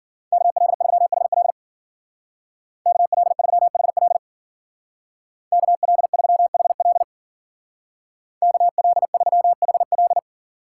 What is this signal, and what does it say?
Morse code, heard as a keyed tone, about 50 wpm, 700 Hz. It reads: KL3HL KL3HL KL3HL KL3HL